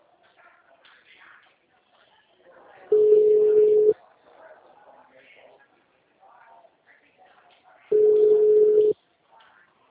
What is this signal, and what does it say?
Ringback tone (Europe, 425 Hz), with speech.